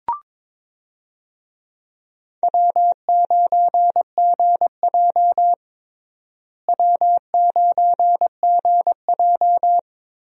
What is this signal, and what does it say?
Morse code, heard as a keyed tone, about 22 wpm, 700 Hz. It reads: W9GJ W9GJ
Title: USA Call Signs - 4 Characters - Repeated 3x - 22wpm